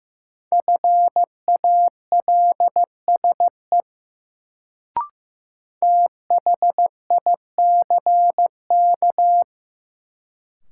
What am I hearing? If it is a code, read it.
Morse code, heard as a keyed tone, about 15 wpm, 700 Hz. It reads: FALSE THICK
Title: Sentences from Top 1600 Words - New Word Drill - 15wpm